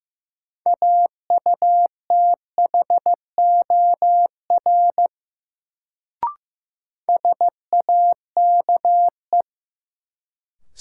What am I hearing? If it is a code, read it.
Morse code, heard as a keyed tone, about 15 wpm, 700 Hz. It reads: AUTHOR SAKE
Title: Sets of 4 Words - New Word Drill - 15wpm